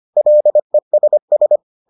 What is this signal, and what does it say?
Morse code, heard as a keyed tone, about 25 wpm, 600 Hz. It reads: LESS